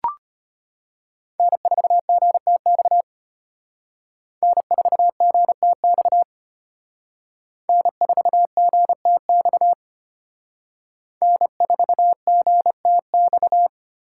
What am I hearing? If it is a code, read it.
Morse code, heard as a keyed tone, about 32 wpm, 700 Hz. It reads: N4GTX N4GTX N4GTX N4GTX